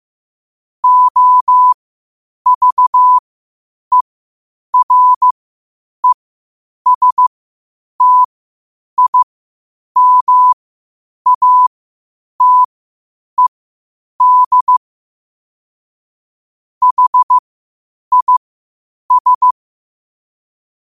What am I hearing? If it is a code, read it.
Morse code, heard as a keyed tone, about 15 wpm, spaced as for 8 wpm, 1000 Hz. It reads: OVERESTIMATED HIS
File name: qotdfiles / 8wpm